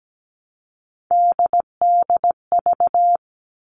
Morse code, heard as a keyed tone, about 17 wpm, 700 Hz. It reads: DDV
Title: All Cut Numbers - 3-Digit 17wpm